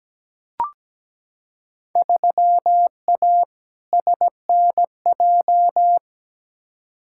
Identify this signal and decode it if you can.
Morse code, heard as a keyed tone, about 17 wpm, 700 Hz. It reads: 3A SNJ
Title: Field Day Exchanges - 17wpm